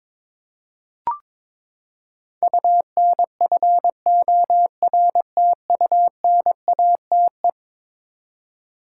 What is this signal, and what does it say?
Morse code, heard as a keyed tone, about 22 wpm, 700 Hz. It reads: UNFORTUNATE
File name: practice / 22wpm / Eleven Letter Words - 22wpm